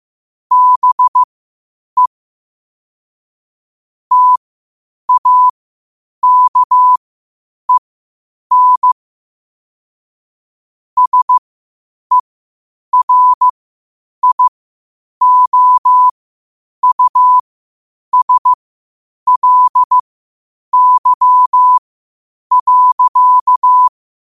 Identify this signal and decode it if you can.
Morse code, heard as a keyed tone, about 15 wpm, spaced as for 8 wpm, 1000 Hz. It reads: BE TAKEN SERIOUSLY.
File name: qotdfiles / 8wpm